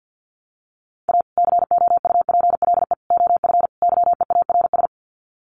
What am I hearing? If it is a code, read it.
Morse code, heard as a keyed tone, about 50 wpm, 700 Hz. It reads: A COUPLE OF YEARS
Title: Sets of 4 Words 50wpm